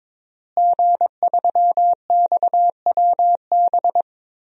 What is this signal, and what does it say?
Morse code, heard as a keyed tone, about 22 wpm, 700 Hz. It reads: G3XWB
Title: DXCC-100 Call Signs - 5 Characters - Repeated 3x - 22wpm